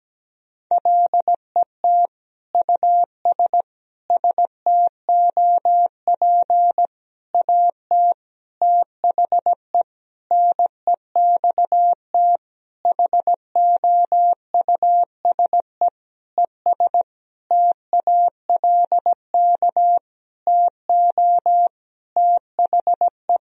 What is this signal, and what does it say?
Morse code, heard as a keyed tone, about 17 wpm, 700 Hz. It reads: LET US STOP AT THE NEXT HOUSE ES TALK TO THE
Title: Sentences from Top 400 Words and Abbreviations 17wpm